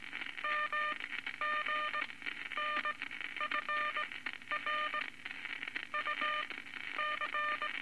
Morse code, heard as a keyed tone, about 18 wpm, spaced as for 11 wpm, 1210 Hz, with noise. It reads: MGNFR UC
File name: Morse Code Records / 19 A LCM 166 Ten Words Per Minute